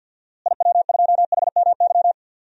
Morse code, heard as a keyed tone, about 46 wpm, 700 Hz. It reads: IW2HMY
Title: DXCC-100 Call Signs - 6 Characters - Repeated 2x - 50wpm